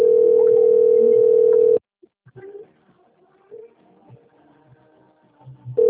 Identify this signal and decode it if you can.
Ringback tone (North America), 440 + 480 Hz.